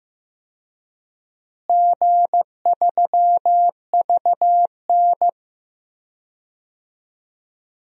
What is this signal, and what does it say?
Morse code, heard as a keyed tone, about 15 wpm, 700 Hz. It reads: G3VN
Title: DXCC-100 Call Signs - 4 Characters - Repeated 2x - 15wpm